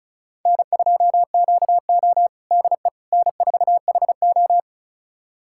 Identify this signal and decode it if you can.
Morse code, heard as a keyed tone, about 34 wpm, 700 Hz. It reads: N2QO DE N4HO
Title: USA Callsign DE Callsign - 4 Characters - 35wpm